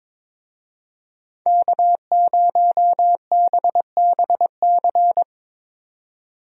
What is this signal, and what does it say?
Morse code, heard as a keyed tone, about 22 wpm, 700 Hz. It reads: K0BBC
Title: USA Call Signs - 5 Characters - 22wpm